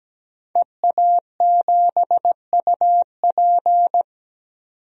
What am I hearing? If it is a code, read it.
Morse code, heard as a keyed tone, about 17 wpm, 700 Hz. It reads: EA7UP
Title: DXCC-100 Call Signs - 5 Characters - 17wpm